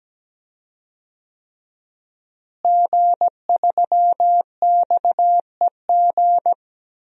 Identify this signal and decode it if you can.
Morse code, heard as a keyed tone, about 17 wpm, 700 Hz. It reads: G3XEG
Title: DXCC-100 Call Signs - 5 Characters - Repeated 3x - 17wpm